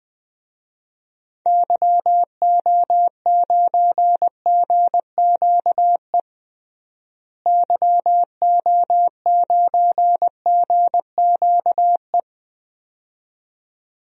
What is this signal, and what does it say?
Morse code, heard as a keyed tone, about 21 wpm, 700 Hz. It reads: YO9GQE YO9GQE
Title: DXCC-100 Call Signs - 6 Characters - Repeated 2x - 20wpm